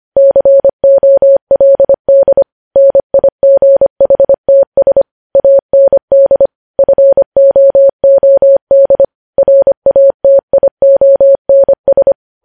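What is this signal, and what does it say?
Morse code, heard as a keyed tone, about 25 wpm, 550 Hz. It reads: COLD NIGHTS AND FOOD RATIONS
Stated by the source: News headlines sent in Morse code at 25 wpm.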